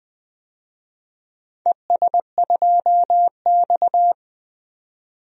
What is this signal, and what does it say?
Morse code, heard as a keyed tone, about 20 wpm, 700 Hz. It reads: ES2X
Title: DXCC-100 Call Signs - 4 Characters - 20wpm